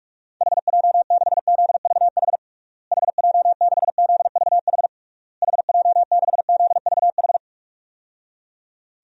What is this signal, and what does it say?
Morse code, heard as a keyed tone, about 46 wpm, 700 Hz. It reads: HJ6ZVH HJ6ZVH HJ6ZVH
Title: DXCC-100 Call Signs - 6 Characters - Repeated 3x - 45wpm